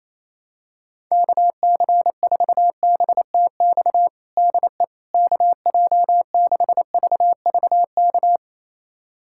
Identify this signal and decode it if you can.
Morse code, heard as a keyed tone, about 27 wpm, 700 Hz. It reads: KC4BTX DE KJ6VVK
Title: USA Callsign DE Callsign - 6 Characters - 28wpm